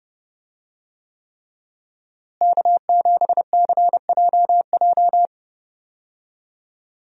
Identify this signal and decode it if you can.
Morse code, heard as a keyed tone, about 30 wpm, 700 Hz. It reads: K7CJJ
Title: USA Call Signs - 5 Characters - Repeated 3x - 30wpm